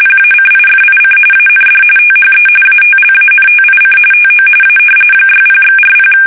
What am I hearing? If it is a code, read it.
RTTY, about 45.45 baud. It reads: PACIFIC AND THE ATLANTIC TO A 60/40
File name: fsk850 / usnavy5